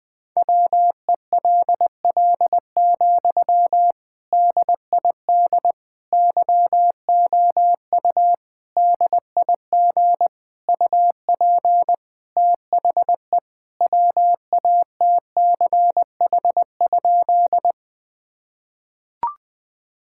Morse code, heard as a keyed tone, about 20 wpm, 700 Hz. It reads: WELL, DID YOU DIG UP THE WATCH?
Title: Sentences from Top 1600 Words 20wpm